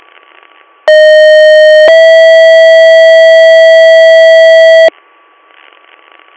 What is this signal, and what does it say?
Two-tone page, heard as a one-second tone, then a three-second tone, about 620 then 650 Hz, with noise. It is heard from 0.9 to 4.9 s.